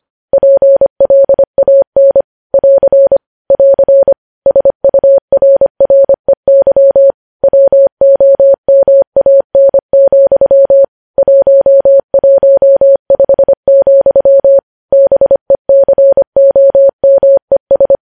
Morse code, heard as a keyed tone, about 25 wpm, 550 Hz. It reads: PLAN <AR> <AR> SURREY WOMAN, 115, BECOMES
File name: podcast / News headlines-1x-25x25